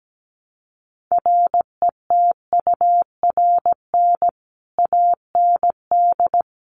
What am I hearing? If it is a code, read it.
Morse code, heard as a keyed tone, about 17 wpm, 700 Hz. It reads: RETURN AND
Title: Sentences from Top 800 Words 17wpm